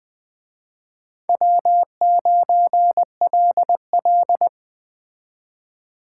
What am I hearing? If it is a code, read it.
Morse code, heard as a keyed tone, about 20 wpm, 700 Hz. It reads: W9LL
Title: Speed Racing - USA 1x2 Call Signs 17wpm